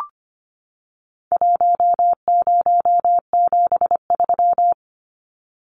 Morse code, heard as a keyed tone, about 25 wpm, 700 Hz. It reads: 1073
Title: Numbers - 4-Digit 25wpm